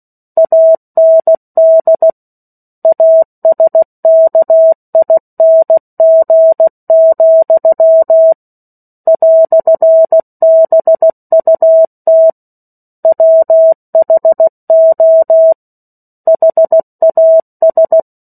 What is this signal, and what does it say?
Morse code, heard as a keyed tone, about 16 wpm, 650 Hz. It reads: AND ASKING, "BUT WHO HAS
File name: 03-Chapter-03 16 WPM